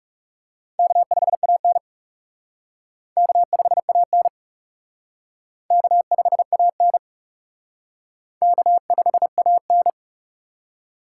Morse code, heard as a keyed tone, about 36 wpm, 700 Hz. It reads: K5AN K5AN K5AN K5AN
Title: Speed Racing - USA 1x2 Call Signs 30wpm